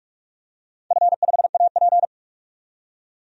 Morse code, heard as a keyed tone, about 44 wpm, 700 Hz. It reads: F5AP
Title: DXCC-100 Call Signs - 4 Characters - 45wpm